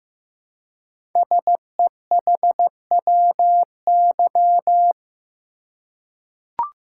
Morse code, heard as a keyed tone, about 15 wpm, 700 Hz. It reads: SEHWY
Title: Code Groups - 5 Letters - 15wpm